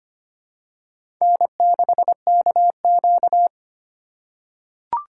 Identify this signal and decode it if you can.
Morse code, heard as a keyed tone, about 25 wpm, 700 Hz. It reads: N6KQ
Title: USA Call Signs - 2x Character Spacing - 25wpm